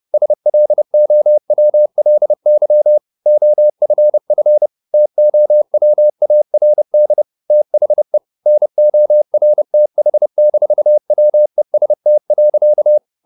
Morse code, heard as a keyed tone, about 30 wpm, 600 Hz. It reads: SLOWLY OFF TOWARD THE NORTH-WEST.
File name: the old man and the sea / the old man and the sea 1 30wpm